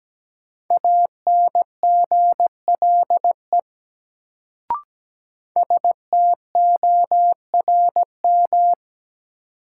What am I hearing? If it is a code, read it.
Morse code, heard as a keyed tone, about 17 wpm, 700 Hz. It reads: ANGLE STORM
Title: Sentences from Top 1500 Words - New Word Drill Alt - 17wpm